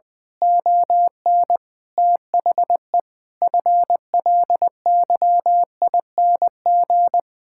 Morse code, heard as a keyed tone, about 20 wpm, 700 Hz. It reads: ON THE FLYING
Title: Pangram Challenge v2 - 20wpm